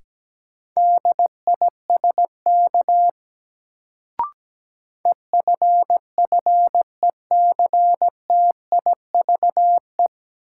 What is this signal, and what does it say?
Morse code, heard as a keyed tone, about 17 wpm, 700 Hz. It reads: DISK EFFECTIVE
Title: Sentences from Top 1000 Words - Review Part 6 - New Word Drill - 17wpm